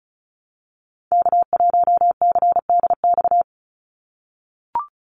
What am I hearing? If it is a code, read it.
Morse code, heard as a keyed tone, about 36 wpm, 700 Hz. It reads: K1CDX
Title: Call Signs - 35wpm